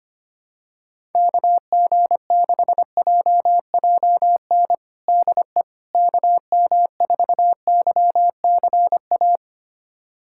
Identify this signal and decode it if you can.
Morse code, heard as a keyed tone, about 25 wpm, 700 Hz. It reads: KG6JJN DE KM4YCA
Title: USA Callsign DE Callsign - 6 Characters - 25wpm